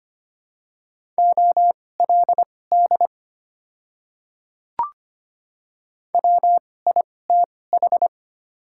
Morse code, heard as a keyed tone, about 25 wpm, spaced as for 18 wpm, 700 Hz. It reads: OLD WITH
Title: 2x Character Spacing - Top 200 Words - 25wpm